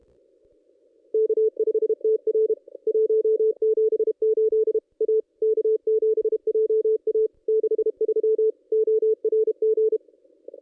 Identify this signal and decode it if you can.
Morse code, heard as a keyed tone, about 32 wpm, 440 Hz. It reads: K5TR 178 A K7JA 63 ORG
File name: contests / ss-cw-k5tr-op-n5rz -1800